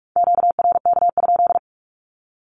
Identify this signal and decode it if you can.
Morse code, heard as a keyed tone, about 44 wpm, 700 Hz. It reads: QRK?
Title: Q-Codes 45wpm